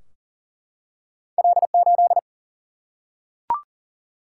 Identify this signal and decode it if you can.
Morse code, heard as a keyed tone, about 39 wpm, 700 Hz. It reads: L8
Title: Single-Letter Single-Number 40wpm